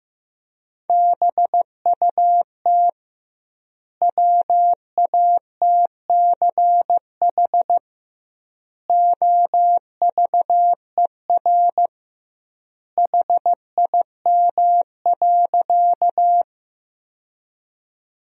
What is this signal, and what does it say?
Morse code, heard as a keyed tone, about 15 wpm, 700 Hz. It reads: BUT WATCH OVER HIM.
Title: 2x Word-Spacing - Sentences from Top 300 Words - 15wpm